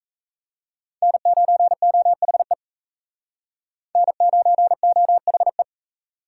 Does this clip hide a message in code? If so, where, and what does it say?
Yes: Morse code from 1.0 to 5.6 s, heard as a keyed tone, about 37 wpm, 700 Hz. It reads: N9OHE N9OHE